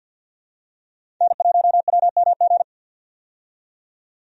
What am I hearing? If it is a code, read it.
Morse code, heard as a keyed tone, about 51 wpm, 700 Hz. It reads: N1WMG
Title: USA Call Signs - 5 Characters - 50wpm